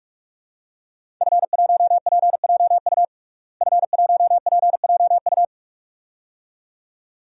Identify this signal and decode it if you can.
Morse code, heard as a keyed tone, about 44 wpm, 700 Hz. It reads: F1PJU F1PJU
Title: DXCC-100 Call Signs - 5 Characters - Repeated 2x - 45wpm